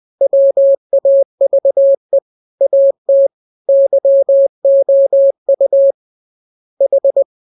Morse code, heard as a keyed tone, about 20 wpm, 550 Hz. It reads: WAVE AT YOU H